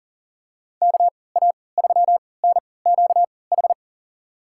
Morse code, heard as a keyed tone, about 40 wpm, spaced as for 23 wpm, 700 Hz. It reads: KA3NQH
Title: USA Call Signs - 6 Characters - 3x Character Spacing - 40wpm